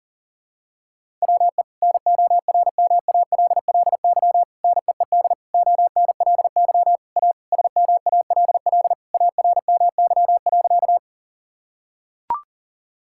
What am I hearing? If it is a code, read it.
Morse code, heard as a keyed tone, about 40 wpm, 700 Hz. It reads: WE NORMALLY NEED ONLY A SMALL ARMY.
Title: Sentences from Top 1000 Words - Review Part 6 - 40wpm